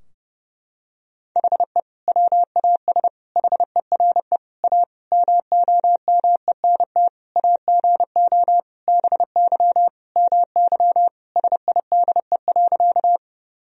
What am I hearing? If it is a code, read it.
Morse code, heard as a keyed tone, about 30 wpm, 700 Hz. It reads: HE WAS HERE A MOMENT AGO BY MY SIDE.
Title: Sentences from Top 1000 Words - Review Part 3 - 30wpm